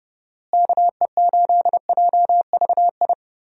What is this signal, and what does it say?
Morse code, heard as a keyed tone, about 32 wpm, 700 Hz. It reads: KE8JVI
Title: USA Call Signs - 6 Characters - Repeated 2x - 30wpm